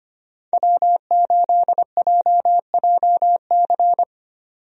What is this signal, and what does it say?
Morse code, heard as a keyed tone, about 25 wpm, 700 Hz. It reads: W8JJC
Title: USA Call Signs - 5 Characters - 25wpm